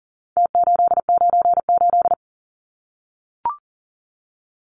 Morse code, heard as a keyed tone, about 40 wpm, 700 Hz. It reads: T898
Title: Numbers - Contest Style - 4-Digit 40wpm